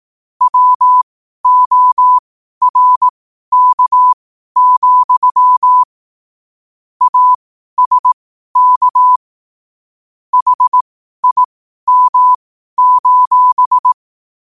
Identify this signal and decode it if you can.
Morse code, heard as a keyed tone, about 18 wpm, spaced as for 12 wpm, 1000 Hz. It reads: WORK, ASK HIM:
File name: 12wpm